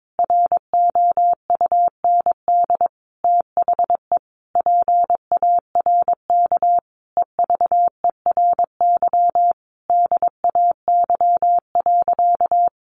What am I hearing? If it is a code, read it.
Morse code, heard as a keyed tone, about 22 wpm, 700 Hz. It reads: ROUND THE PARK EVERY DAY.
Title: Sentences from Top 1300 Words 22wpm